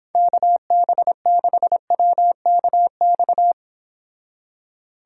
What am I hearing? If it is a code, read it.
Morse code, heard as a keyed tone, about 31 wpm, 700 Hz. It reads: KB6WKX